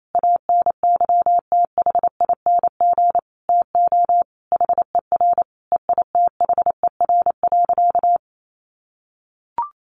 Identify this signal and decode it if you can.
Morse code, heard as a keyed tone, about 28 wpm, 700 Hz. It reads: ANYTHING TO HER EITHER.
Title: Sentences from Top 1000 Words 28wpm